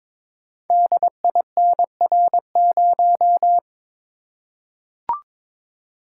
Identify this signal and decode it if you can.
Morse code, heard as a keyed tone, about 22 wpm, 700 Hz. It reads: DINR0
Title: Code Groups - 5 Letter:Numbers - 22wpm